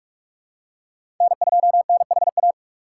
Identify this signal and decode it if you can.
Morse code, heard as a keyed tone, about 46 wpm, 700 Hz. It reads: N2NHA
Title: USA Call Signs - 2x Character Spacing - 45wpm